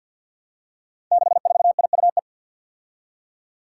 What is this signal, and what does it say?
Morse code, heard as a keyed tone, about 47 wpm, 700 Hz. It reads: 64IUE
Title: Five Challenging Letter-Number Combinations 50wpm